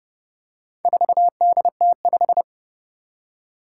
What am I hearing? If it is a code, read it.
Morse code, heard as a keyed tone, about 30 wpm, 700 Hz. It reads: 4DT5
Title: Four Challenging Letter-Number Combinations 30wpm